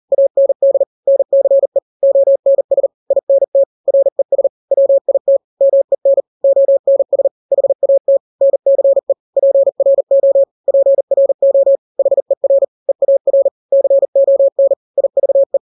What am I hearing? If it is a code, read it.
Morse code, heard as a keyed tone, about 40 wpm, 550 Hz. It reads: AND NCE ONS INT RES WIT MEN ONS HAT NCE PRO PRO HER EAR CON IVE
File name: n-grams / 3-grams 40 35 WPM